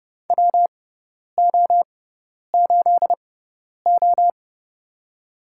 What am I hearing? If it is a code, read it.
Morse code, heard as a keyed tone, about 30 wpm, spaced as for 10 wpm, 700 Hz. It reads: WO8O
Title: USA Call Signs - 4 Characters - 6x Character Spacing - 30wpm